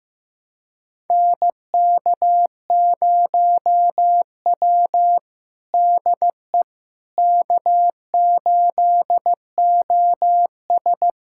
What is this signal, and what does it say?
Morse code, heard as a keyed tone, about 15 wpm, 700 Hz. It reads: NK0W DE K8OS
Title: USA Callsign DE Callsign - 4 Characters - 15wpm